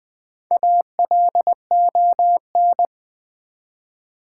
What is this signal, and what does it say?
Morse code, heard as a keyed tone, about 20 wpm, 700 Hz. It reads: ALON
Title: Build A Word - 1 Letter At A Time - 5-Letter Words - 20wpm